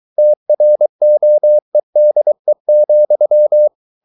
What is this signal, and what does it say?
Morse code, heard as a keyed tone, about 23 wpm, 600 Hz. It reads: TROEDE,
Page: Ferie 23 wpm